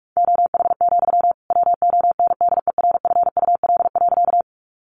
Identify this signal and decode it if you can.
Morse code, heard as a keyed tone, about 44 wpm, 700 Hz. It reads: OH, WONDERFUL.
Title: Sentences from Top 1100 Words 45wpm